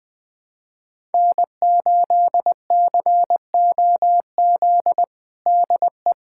Morse code, heard as a keyed tone, about 20 wpm, 700 Hz. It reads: N8COZ DE
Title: USA Callsign DE Callsign - 5 Characters - 20wpm